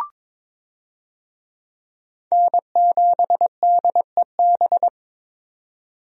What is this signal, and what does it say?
Morse code, heard as a keyed tone, about 22 wpm, 700 Hz. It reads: N7DEB
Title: USA Call Signs - 5 Characters - 22wpm